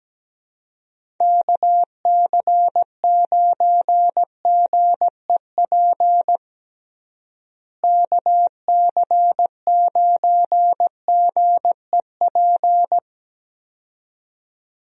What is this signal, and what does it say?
Morse code, heard as a keyed tone, about 17 wpm, 700 Hz. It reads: KC9GEP KC9GEP
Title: USA Call Signs - 6 Characters - Repeated 2x - 17wpm